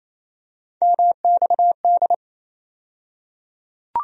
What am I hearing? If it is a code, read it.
Morse code, heard as a keyed tone, about 28 wpm, 700 Hz. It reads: MXD
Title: USA Call Sign Postfixes - 3 Letters 28wpm